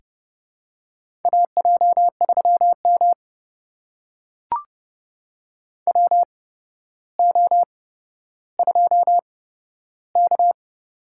Morse code, heard as a keyed tone, about 30 wpm, spaced as for 13 wpm, 700 Hz. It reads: AJ3M WO2K
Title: USA Call Signs - 4 Characters - 8x Character Spacing - 30wpm